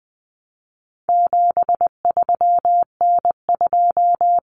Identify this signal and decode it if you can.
Morse code, heard as a keyed tone, about 20 wpm, 700 Hz. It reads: 73N2
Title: Common Cut Numbers - 4-Digit 20wpm